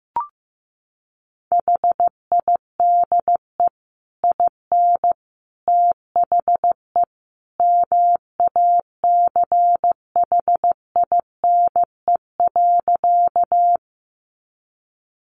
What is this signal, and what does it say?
Morse code, heard as a keyed tone, about 15 wpm, 700 Hz. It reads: HIDE IN THE MACHINE.
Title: A Sound Of Thunder 15wpm